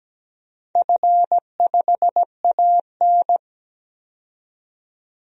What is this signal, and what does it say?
Morse code, heard as a keyed tone, about 17 wpm, 700 Hz. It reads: F5AN
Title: DXCC-100 Call Signs - 4 Characters - 17wpm